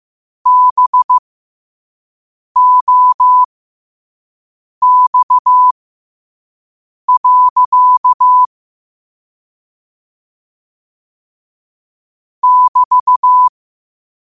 Morse code, heard as a keyed tone, about 15 wpm, spaced as for 5 wpm, 1000 Hz. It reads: BOX. <BT>
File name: qotdfiles / 5wpm